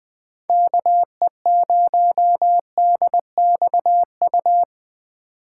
Morse code, heard as a keyed tone, about 20 wpm, 700 Hz. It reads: KE0DXU
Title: USA Call Signs - 6 Characters - 3x Character Spacing - 20wpm